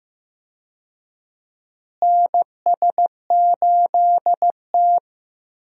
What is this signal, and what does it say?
Morse code, heard as a keyed tone, about 15 wpm, 700 Hz. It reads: NS8T
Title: USA Call Signs - 4 Characters - Repeated 2x - 15wpm